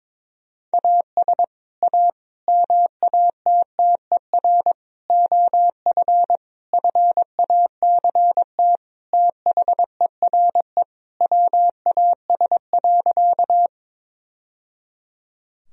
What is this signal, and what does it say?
Morse code, heard as a keyed tone, about 21 wpm, 700 Hz. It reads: AS A MATTER OF FACT THERE WAS.
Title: Sentences from Top 1000 Words - Review Part 3 - 22wpm